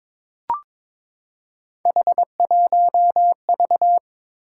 Morse code, heard as a keyed tone, about 23 wpm, 700 Hz. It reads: H1V
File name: practice / 22wpm / Three Letter-Number Combinations - 22wpm